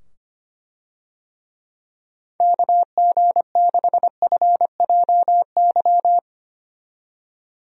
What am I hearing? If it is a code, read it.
Morse code, heard as a keyed tone, about 25 wpm, 700 Hz. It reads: KG6FJY
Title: USA Call Signs - 6 Characters - 25wpm